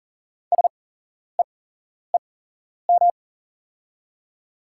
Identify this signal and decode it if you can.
Morse code, heard as a keyed tone, about 38 wpm, spaced as for 11 wpm, 700 Hz. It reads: SEEM
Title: 8x Character Spacing - Top 300 Words - 40wpm